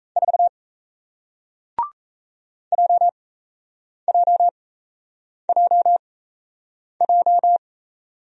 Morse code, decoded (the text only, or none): 4 JJJJ